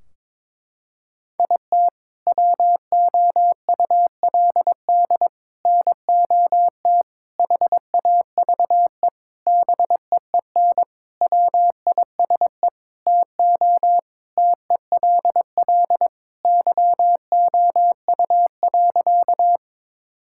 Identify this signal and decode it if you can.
Morse code, heard as a keyed tone, about 22 wpm, 700 Hz. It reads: IT WOULD NOT HAVE BEEN WISE TO TELL YOU.
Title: Sentences from Top 1600 Words 22wpm